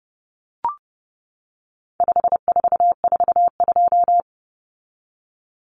Morse code, heard as a keyed tone, about 30 wpm, 700 Hz. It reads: 5442
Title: Numbers - 4-Digit 30wpm